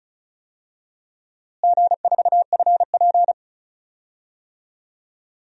Morse code, heard as a keyed tone, about 34 wpm, 700 Hz. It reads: G4FP
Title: DXCC-100 Call Signs - 4 Characters - Repeated 3x - 35wpm